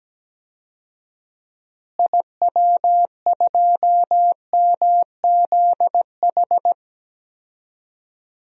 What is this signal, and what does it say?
Morse code, heard as a keyed tone, about 17 wpm, 700 Hz. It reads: IW2MZH
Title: DXCC-100 Call Signs - 6 Characters - Repeated 3x - 17wpm